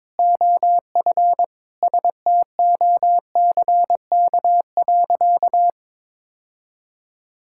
Morse code, heard as a keyed tone, about 22 wpm, 700 Hz. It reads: OF STOCK.
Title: Sentences from Top 1000 Words - Review Part 5 - 22wpm